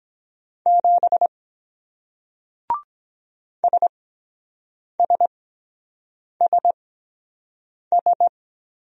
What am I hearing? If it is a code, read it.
Morse code, heard as a keyed tone, about 23 wpm, spaced as for 7 wpm, 700 Hz. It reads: 7 SSSS